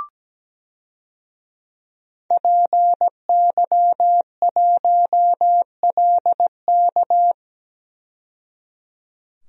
Morse code, heard as a keyed tone, about 17 wpm, 700 Hz. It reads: PY1LK
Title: DXCC-100 Call Signs - 5 Characters - 17wpm